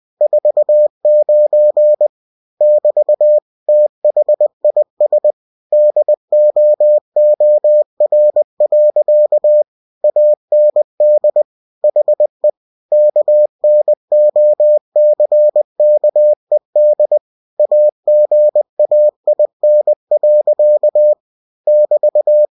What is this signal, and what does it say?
Morse code, heard as a keyed tone, about 20 wpm, 600 Hz. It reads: 49 <BT> THIS DOOR. AND HE KNOCKED AGAIN. <BT>
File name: cw-books / winnie-the-pooh / winnie-20wpm-page-0049